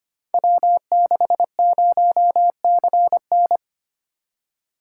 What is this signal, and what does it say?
Morse code, heard as a keyed tone, about 25 wpm, 700 Hz. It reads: W60CN
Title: Code Groups - 5 Letter:Numbers - 25wpm